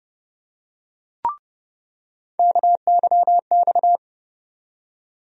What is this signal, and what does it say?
Morse code, heard as a keyed tone, about 30 wpm, 700 Hz. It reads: KYX
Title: USA Call Sign Postfixes - 3 Letters 30wpm